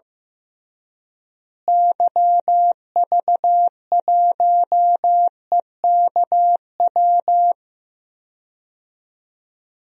Morse code, heard as a keyed tone, about 15 wpm, 700 Hz. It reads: YV1EKW
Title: DXCC-100 Call Signs - 6 Characters - Repeated 3x - 15wpm